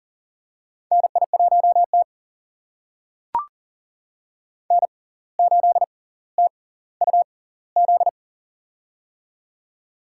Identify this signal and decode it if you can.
Morse code, heard as a keyed tone, about 40 wpm, spaced as for 19 wpm, 700 Hz. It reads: NI1T N8TUZ
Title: USA Call Signs - 6x Character Spacing - 40wpm